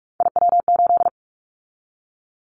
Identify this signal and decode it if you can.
Morse code, heard as a keyed tone, about 46 wpm, 700 Hz. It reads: IW8
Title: Callsign Prefixes for Top 25 DXCC - 45wpm